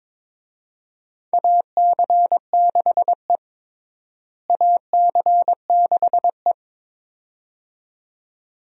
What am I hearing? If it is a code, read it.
Morse code, heard as a keyed tone, about 22 wpm, 700 Hz. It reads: AC6E AC6E
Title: USA Call Signs - 4 Characters - Repeated 2x - 22wpm